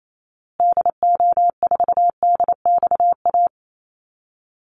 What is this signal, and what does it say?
Morse code, heard as a keyed tone, about 28 wpm, 700 Hz. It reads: DO4DXA
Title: Call Signs Repeated 3x - 28wpm